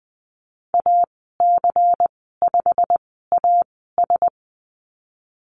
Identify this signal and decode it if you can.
Morse code, heard as a keyed tone, about 20 wpm, spaced as for 15 wpm, 700 Hz. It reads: AC5AS
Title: 154 Lesson 25g - Letter p - Call Signs 20wpm